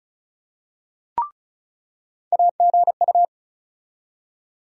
Morse code, heard as a keyed tone, about 36 wpm, 700 Hz. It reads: AGU
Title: USA Call Sign Postfixes - 3 Letters 35wpm